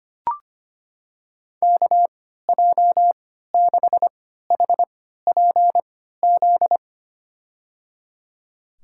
Morse code, heard as a keyed tone, about 25 wpm, spaced as for 14 wpm, 700 Hz. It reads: KJ6HPZ
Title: USA Call Signs - 6 Characters - 3x Character Spacing - 25wpm